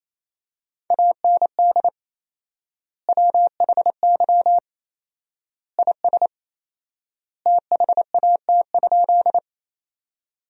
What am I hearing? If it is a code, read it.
Morse code, heard as a keyed tone, about 28 wpm, 700 Hz. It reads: AND WHY IS THAT?
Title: 4x Word-Spacing - Sentences from Top 200 Words - 28wpm